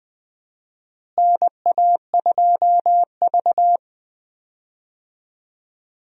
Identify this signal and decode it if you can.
Morse code, heard as a keyed tone, about 20 wpm, 700 Hz. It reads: NA2V
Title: USA Call Signs - 4 Characters - Repeated 3x - 20wpm